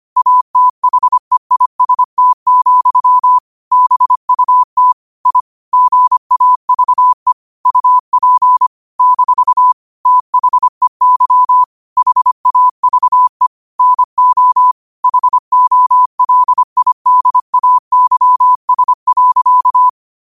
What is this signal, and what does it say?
Morse code, heard as a keyed tone, about 25 wpm, 1000 Hz. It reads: ATHEIST, BUT I GAVE UP - THEY HAVE NO HOLIDAYS.
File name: 25wpm